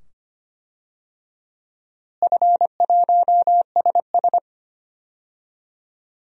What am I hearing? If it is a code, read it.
Morse code, heard as a keyed tone, about 25 wpm, 700 Hz. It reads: F1SS
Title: DXCC-100 Call Signs - 4 Characters - Repeated 2x - 25wpm